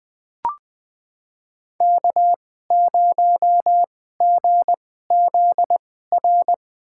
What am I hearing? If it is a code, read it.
Morse code, heard as a keyed tone, about 21 wpm, spaced as for 15 wpm, 700 Hz. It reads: K0GZR
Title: USA Call Signs - 5 Characters - 2x Character Spacing - 20wpm